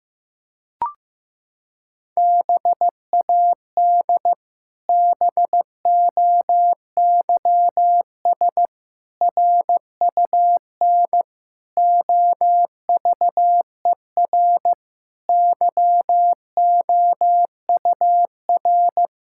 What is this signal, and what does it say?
Morse code, heard as a keyed tone, about 16 wpm, 700 Hz. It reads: BAD BOYS RUN OVER YOUR
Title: Pangram Challenge v2 - 15wpm